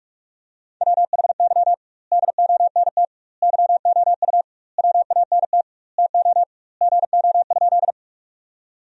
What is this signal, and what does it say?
Morse code, heard as a keyed tone, about 45 wpm, 700 Hz. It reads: WHY DONT YOU WANT TO GO?
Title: 2x Word-Spacing - Sentences from Top 300 Words - 45wpm